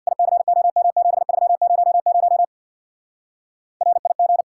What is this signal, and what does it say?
Morse code, heard as a keyed tone, about 64 wpm, 700 Hz. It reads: ICOM7300 RIG